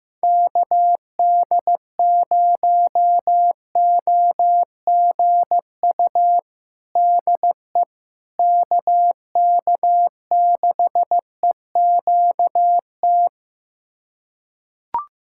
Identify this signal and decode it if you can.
Morse code, heard as a keyed tone, about 15 wpm, 700 Hz. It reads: KD0OGU DE KK6EQT
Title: USA Callsign DE Callsign - 6 Characters - 15wpm